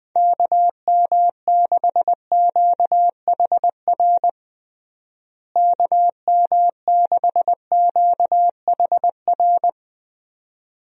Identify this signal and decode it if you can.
Morse code, heard as a keyed tone, about 20 wpm, 700 Hz. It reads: KM6QHR KM6QHR
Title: USA Call Signs - 6 Characters - Repeated 2x - 20wpm